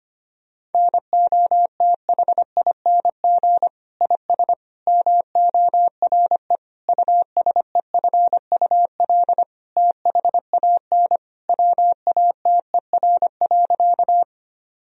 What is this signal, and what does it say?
Morse code, heard as a keyed tone, about 25 wpm, 700 Hz. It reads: NOTHING IS MORE USEFUL THAN WATER.
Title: Sentences from Top 1000 Words - Review Part 4 - 25wpm